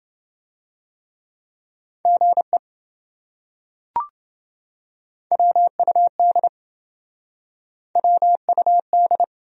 Morse code, heard as a keyed tone, about 27 wpm, 700 Hz. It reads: GE WUD WUD